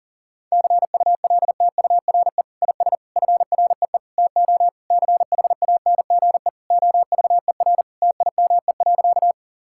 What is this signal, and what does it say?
Morse code, heard as a keyed tone, about 40 wpm, 700 Hz. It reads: CULTURE IS FREE TO CHANGE OVER TIME.
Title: Sentences from Top 1000 Words - Review Part 6 - 40wpm